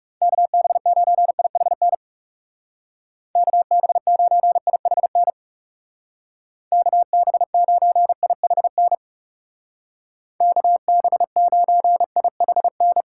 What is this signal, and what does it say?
Morse code, heard as a keyed tone, about 39 wpm, 700 Hz. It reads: KB9IHN KB9IHN KB9IHN KB9IHN